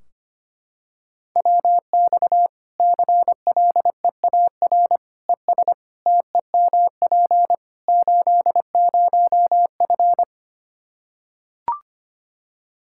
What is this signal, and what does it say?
Morse code, heard as a keyed tone, about 25 wpm, 700 Hz. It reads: WX CLEAR ES TEMP 80F
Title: K1USN SST QSO Practice 25wpm